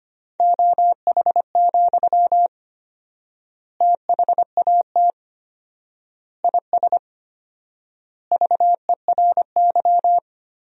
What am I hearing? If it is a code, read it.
Morse code, heard as a keyed tone, about 25 wpm, 700 Hz. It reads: OH, THAT IS VERY
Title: 4x Word-Spacing - Sentences from Top 500 Words - 25wpm